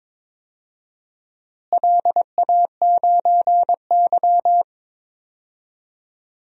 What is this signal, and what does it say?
Morse code, heard as a keyed tone, about 22 wpm, 700 Hz. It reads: LA9Y
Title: DXCC-100 Call Signs - 4 Characters - Repeated 2x - 22wpm